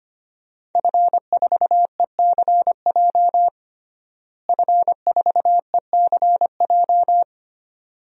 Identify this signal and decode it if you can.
Morse code, heard as a keyed tone, about 25 wpm, 700 Hz. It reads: F4ECJ F4ECJ
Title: DXCC-100 Call Signs - 5 Characters - Repeated 2x - 25wpm